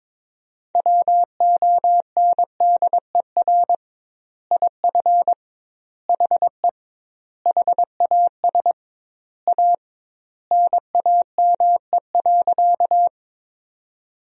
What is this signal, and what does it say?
Morse code, heard as a keyed tone, about 21 wpm, 700 Hz. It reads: WONDER IF HE HAS A NAME.
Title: 2x Word-Spacing - Sentences from Top 500 Words - 22wpm